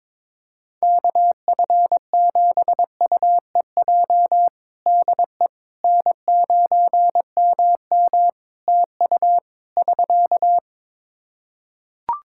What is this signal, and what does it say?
Morse code, heard as a keyed tone, about 22 wpm, 700 Hz. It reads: KF7UEJ DE N9MM TU <SK>
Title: Practice QSO - Exchange 4 - Ending - 22wpm